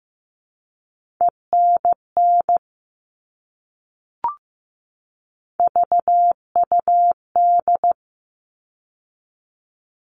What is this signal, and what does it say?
Morse code, heard as a keyed tone, about 15 wpm, 700 Hz. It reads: ENN VUD
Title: All Cut Numbers - 3-Digit 15wpm